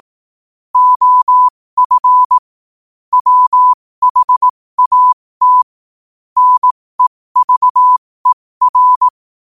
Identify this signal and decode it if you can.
Morse code, heard as a keyed tone, about 18 wpm, spaced as for 15 wpm, 1000 Hz. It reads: OF WHAT NEVER
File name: qotdfiles / 15wpm